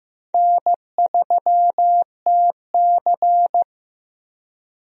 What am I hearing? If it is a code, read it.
Morse code, heard as a keyed tone, about 15 wpm, 700 Hz. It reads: N3TC
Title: USA Call Signs - 4 Characters - Repeated 3x - 15wpm